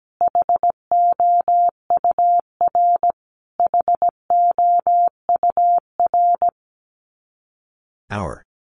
Morse code, heard as a keyed tone, about 17 wpm, 700 Hz. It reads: HOUR HOUR
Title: Top 500 Words Repeated 17wpm